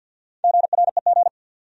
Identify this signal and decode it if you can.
Morse code, heard as a keyed tone, about 52 wpm, 700 Hz. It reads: GREG
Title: USA Common Names - Max 5 Letters - 50wpm